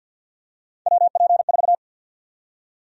Morse code, heard as a keyed tone, about 49 wpm, 700 Hz. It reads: WP4
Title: USA Call Sign Prefixes - 50wpm